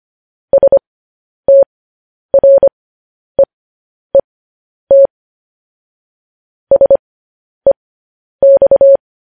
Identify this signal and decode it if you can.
Morse code, heard as a keyed tone, about 25 wpm, spaced as for 10 wpm, 550 Hz. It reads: STREET SEX
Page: news-headlines-in-morse-code-at-10-wpm